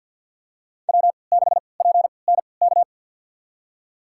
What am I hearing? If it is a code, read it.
Morse code, heard as a keyed tone, about 51 wpm, spaced as for 28 wpm, 700 Hz. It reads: W6PNK
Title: USA Call Signs - 5 Characters - 3x Character Spacing - 50wpm